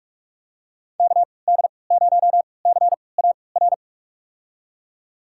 Morse code, heard as a keyed tone, about 46 wpm, spaced as for 25 wpm, 700 Hz. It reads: KD0CAR
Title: USA Call Signs - 6 Characters - 3x Character Spacing - 45wpm